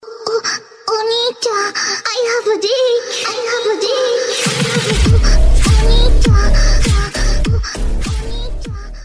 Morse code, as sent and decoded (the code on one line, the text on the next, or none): none